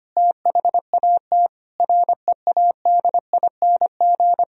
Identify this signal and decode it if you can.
Morse code, heard as a keyed tone, about 25 wpm, 700 Hz. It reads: THAT READING
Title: Sentences from Top 1000 Words - Review Part 3 - 25wpm